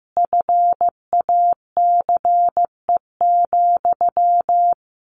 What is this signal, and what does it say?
Morse code, heard as a keyed tone, about 15 wpm, 700 Hz. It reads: FACE,
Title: A Sound Of Thunder 15wpm